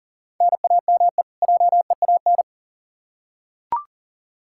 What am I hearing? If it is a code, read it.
Morse code, heard as a keyed tone, about 40 wpm, 700 Hz. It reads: NAME JEAN
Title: Practice QSO - Exchange 1 - Beginning - 40wpm